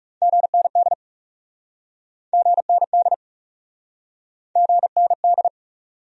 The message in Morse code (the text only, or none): GND GND GND